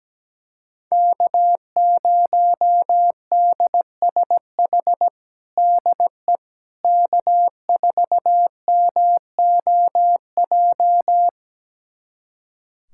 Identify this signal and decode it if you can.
Morse code, heard as a keyed tone, about 17 wpm, 700 Hz. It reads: K0DSH DE K4MOJ
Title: USA Callsign DE Callsign - 5 Characters - 17wpm